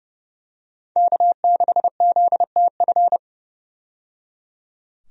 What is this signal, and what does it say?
Morse code, heard as a keyed tone, about 30 wpm, 700 Hz. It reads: K6ZTF
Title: USA Call Signs - 5 Characters - 30wpm